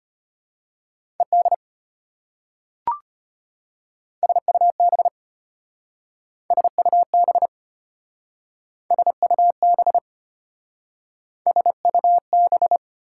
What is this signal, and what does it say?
Morse code, heard as a keyed tone, about 32 wpm, 700 Hz. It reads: ED SUB SUB SUB SUB